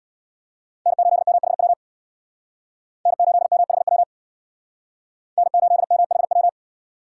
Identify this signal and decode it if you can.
Morse code, heard as a keyed tone, about 62 wpm, 700 Hz. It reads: N8MHW N8MHW N8MHW
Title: Speed Racing - USA 1x3 Call Signs 50wpm